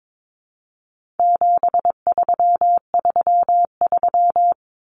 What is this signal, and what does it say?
Morse code, heard as a keyed tone, about 22 wpm, 700 Hz. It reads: 7333
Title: Numbers - 4-Digit 22wpm